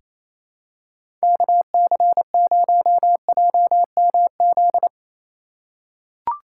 Morse code, heard as a keyed tone, about 27 wpm, 700 Hz. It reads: KC0JMZ
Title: USA Call Signs - 6 Characters - 3x Character Spacing - 28wpm